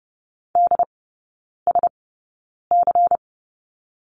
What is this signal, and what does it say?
Morse code, heard as a keyed tone, about 30 wpm, spaced as for 9 wpm, 690 Hz. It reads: DSC
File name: Lesson 16e - Letter c - Three Characters 30wpm